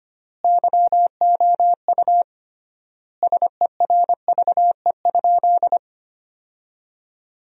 Morse code, heard as a keyed tone, about 25 wpm, 700 Hz. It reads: YOU SERVE?
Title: 3x Word-Spacing - Sentences from Top 400 Words - 25wpm